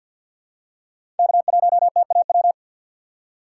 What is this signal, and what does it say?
Morse code, heard as a keyed tone, about 53 wpm, 700 Hz. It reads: K1TAW
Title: USA Call Signs - 5 Characters - 2x Character Spacing - 50wpm